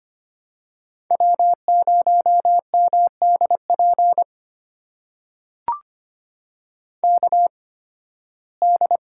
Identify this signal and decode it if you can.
Morse code, heard as a keyed tone, about 25 wpm, spaced as for 16 wpm, 700 Hz. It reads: W0MDP KD
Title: USA Call Signs - 5 Characters - 8x Character Spacing - 25wpm